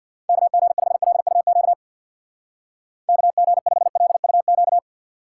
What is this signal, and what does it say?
Morse code, heard as a keyed tone, about 66 wpm, 700 Hz. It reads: KG5LUQ KG5LUQ